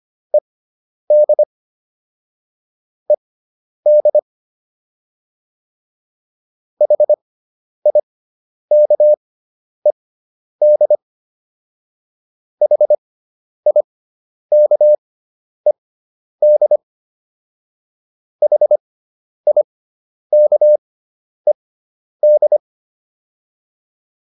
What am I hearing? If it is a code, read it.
Morse code, heard as a keyed tone, about 25 wpm, spaced as for 10 wpm, 600 Hz. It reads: ED ED HIKED HIKED HIKED
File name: ED5 10